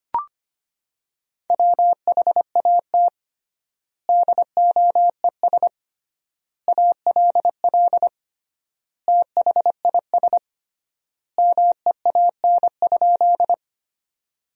Morse code, heard as a keyed tone, about 25 wpm, 700 Hz. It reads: WHAT DOES ALL THIS MEAN?
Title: 3x Word-Spacing - Sentences from Top 200 Words - 25wpm